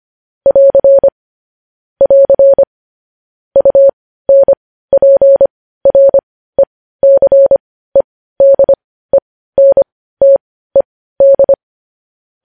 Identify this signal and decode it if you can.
Morse code, heard as a keyed tone, about 25 wpm, spaced as for 15 wpm, 550 Hz. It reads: <AR> <AR> UNPRECEDENTED
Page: News Headlines in Morse Code at 15 WPM